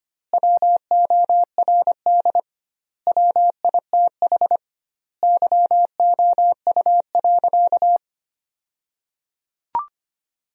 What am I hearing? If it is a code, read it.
Morse code, heard as a keyed tone, about 25 wpm, 700 Hz. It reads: WORD WITH YOU.
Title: 2x Word-Spacing - Sentences from Top 300 Words - 25wpm